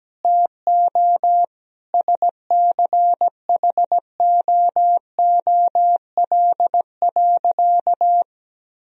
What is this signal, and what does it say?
Morse code, heard as a keyed tone, about 17 wpm, 700 Hz. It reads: TO SCHOOL.
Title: Sentences from Top 1000 Words - Review Part 1 - 17wpm